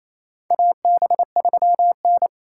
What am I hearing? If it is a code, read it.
Morse code, heard as a keyed tone, about 28 wpm, 700 Hz. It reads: AB3N
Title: USA Call Signs - 6x Character Spacing - 28wpm